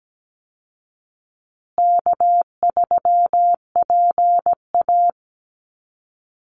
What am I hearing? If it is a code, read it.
Morse code, heard as a keyed tone, about 17 wpm, 700 Hz. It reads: K3PA
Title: Easy Call Signs Repeated 3x - 17wpm